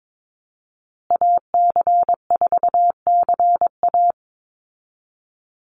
Morse code, heard as a keyed tone, about 22 wpm, 700 Hz. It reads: AC4CA
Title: Call Signs Repeated 3x - 22wpm